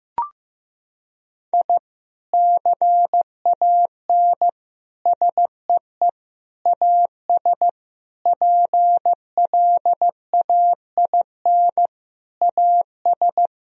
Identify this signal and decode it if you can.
Morse code, heard as a keyed tone, about 15 wpm, 700 Hz. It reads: I CAN SEE AS PLAIN AS
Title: Sentences from Top 1000 Words - Review Part 2 - 15wpm